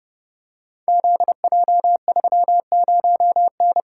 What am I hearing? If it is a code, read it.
Morse code, heard as a keyed tone, about 30 wpm, 700 Hz. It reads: ZJ30N
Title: Code Groups - 5 Letter:Numbers - 30wpm